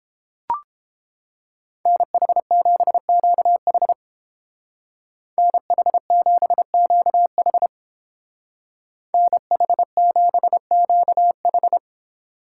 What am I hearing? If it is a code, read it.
Morse code, heard as a keyed tone, about 28 wpm, 700 Hz. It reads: NH7QH NH7QH NH7QH